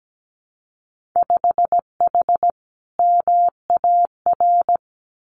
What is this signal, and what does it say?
Morse code, heard as a keyed tone, about 17 wpm, 700 Hz. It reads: 5H MAR
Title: Winter Field Day Exchanges Encore 17wpm